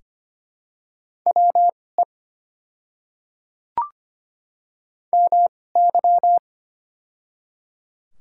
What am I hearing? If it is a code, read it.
Morse code, heard as a keyed tone, about 25 wpm, spaced as for 18 wpm, 700 Hz. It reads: WE MY
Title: 2x Character Spacing - Top 200 Words - 25wpm